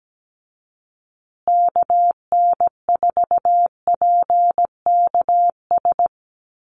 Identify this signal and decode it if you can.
Morse code, heard as a keyed tone, about 17 wpm, 700 Hz. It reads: KN4PKS
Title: USA Call Signs Repeated 2x 17wpm